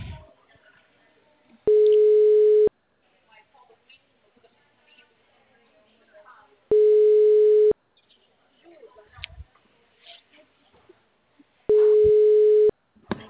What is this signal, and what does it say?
Ringback tone (Europe, 425 Hz), with speech.